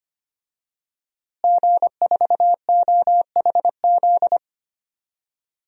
Morse code, heard as a keyed tone, about 25 wpm, 700 Hz. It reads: G4OHZ
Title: DXCC-100 Call Signs - 5 Characters - Repeated 3x - 25wpm